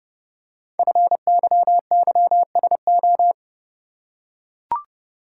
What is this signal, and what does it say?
Morse code, heard as a keyed tone, about 30 wpm, 700 Hz. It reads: FYYSO
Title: Code Groups - 5 Letters - 30wpm